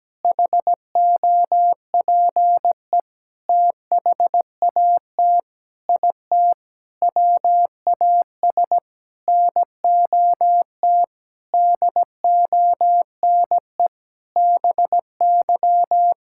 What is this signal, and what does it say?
Morse code, heard as a keyed tone, about 17 wpm, 700 Hz. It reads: HOPE THAT IT WAS NOT DONE BY
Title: Sentences from Top 1000 Words - Review Part 5 - 17wpm